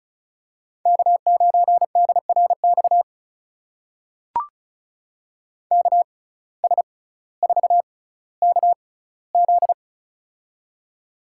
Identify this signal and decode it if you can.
Morse code, heard as a keyed tone, about 34 wpm, spaced as for 18 wpm, 700 Hz. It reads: K9DRX KS4KZ
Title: USA Call Signs - 6x Character Spacing - 35wpm